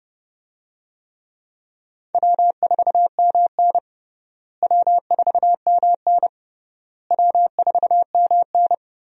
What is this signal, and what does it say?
Morse code, heard as a keyed tone, about 32 wpm, 700 Hz. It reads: W4MN W4MN W4MN
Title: USA Call Signs - 4 Characters - Repeated 3x - 30wpm